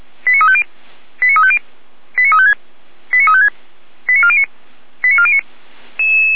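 Five-tone selective call (ZVEI), heard as quick runs of five tones, tones of about 70 ms. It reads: 89350 89350 89356 89356 89409 89409
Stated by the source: Five-tone ZVEI selective call received on an SDR, from audio.f85360.feuerwehr-orlamuende.org (Feuerwehr Orlamünde fire brigade).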